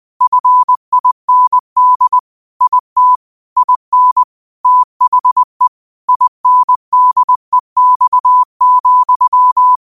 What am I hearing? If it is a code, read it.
Morse code, heard as a keyed tone, about 20 wpm, 1000 Hz. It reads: FIND IT IN THE INDEX,